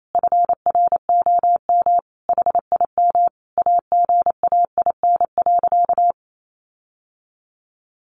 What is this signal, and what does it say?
Morse code, heard as a keyed tone, about 28 wpm, 700 Hz. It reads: FROM HIM AGAIN.
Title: Sentences from Top 1300 Words 28wpm